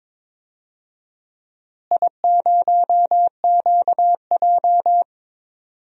Morse code, heard as a keyed tone, about 22 wpm, 700 Hz. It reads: I0QJ
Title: DXCC-100 Call Signs - 4 Characters - Repeated 2x - 22wpm